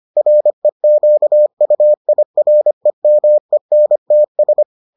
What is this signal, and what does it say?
Morse code, heard as a keyed tone, about 25 wpm, 600 Hz. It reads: REQUIREMENTS